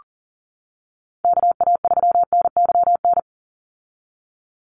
Morse code, heard as a keyed tone, about 40 wpm, 700 Hz. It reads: KA3NYN
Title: USA Call Signs 40wpm